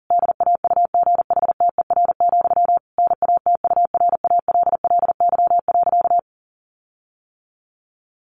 Morse code, heard as a keyed tone, about 40 wpm, 700 Hz. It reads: DAUGHTER, NATURALLY.
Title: Sentences from Top 1500 Words 40wpm